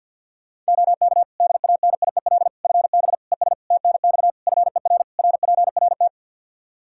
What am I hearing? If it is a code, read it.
Morse code, heard as a keyed tone, about 51 wpm, 700 Hz. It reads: OK DANIEL FB ES TNX FER RPRT
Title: K1USN SST QSO Practice 50wpm